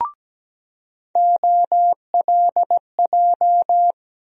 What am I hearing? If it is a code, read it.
Morse code, heard as a keyed tone, about 17 wpm, 700 Hz. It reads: OLJ
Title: USA Call Sign Postfixes - 3 Letters 17wpm